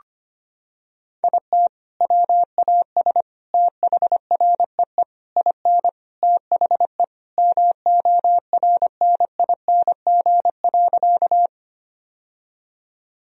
Morse code, heard as a keyed tone, about 25 wpm, 700 Hz. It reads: IT WAS THREE IN THE MORNING.
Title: Sentences from Top 1000 Words - Review Part 1 - 25wpm